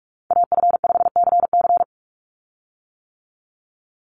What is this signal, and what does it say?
Morse code, heard as a keyed tone, about 44 wpm, 700 Hz. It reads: AF5CC
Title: Call Signs - 45wpm